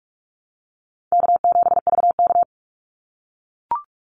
Morse code, heard as a keyed tone, about 44 wpm, 700 Hz. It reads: K7VK
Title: NA SOTA Activator Callsigns 45wpm